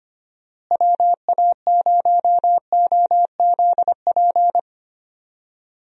Morse code, heard as a keyed tone, about 25 wpm, 700 Hz. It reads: WA0OZP
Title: USA Call Signs - 6 Characters - Repeated 2x - 25wpm